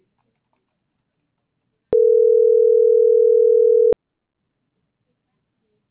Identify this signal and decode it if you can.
Ringback tone (North America), 440 + 480 Hz.